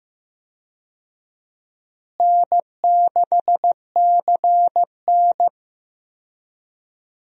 Morse code, heard as a keyed tone, about 15 wpm, 700 Hz. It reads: N6CN
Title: USA Call Signs - 4 Characters - Repeated 3x - 15wpm